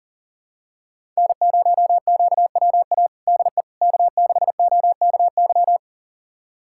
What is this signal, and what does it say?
Morse code, heard as a keyed tone, about 40 wpm, 700 Hz. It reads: N0QWA DE K6OKY
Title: USA Callsign DE Callsign - 5 Characters - 40wpm